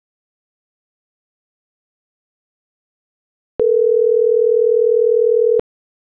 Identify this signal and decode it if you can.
Ringback tone (North America), 440 + 480 Hz.